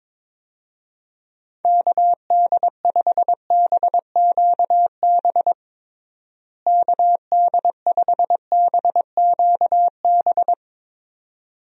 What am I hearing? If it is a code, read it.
Morse code, heard as a keyed tone, about 22 wpm, 700 Hz. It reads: KD5BQB KD5BQB
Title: USA Call Signs - 6 Characters - Repeated 2x - 22wpm